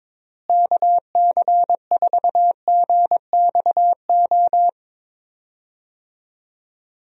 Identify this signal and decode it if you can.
Morse code, heard as a keyed tone, about 22 wpm, 700 Hz. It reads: KC4GXO
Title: USA Call Signs - 6 Characters - Repeated 3x - 22wpm